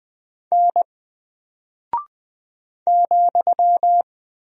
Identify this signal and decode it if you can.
Morse code, heard as a keyed tone, about 20 wpm, 700 Hz. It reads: N ,
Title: Single Character (Letter:Number:Punctuation:Prosign) Alt 20wpm